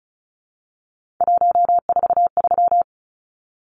Morse code, heard as a keyed tone, about 34 wpm, 700 Hz. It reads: 143
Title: Numbers - 3-Digit 35wpm